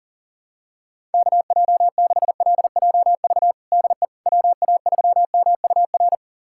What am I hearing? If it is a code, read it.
Morse code, heard as a keyed tone, about 40 wpm, 700 Hz. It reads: KJ6LJV DE WA3MUR
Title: USA Callsign DE Callsign - 6 Characters - 40wpm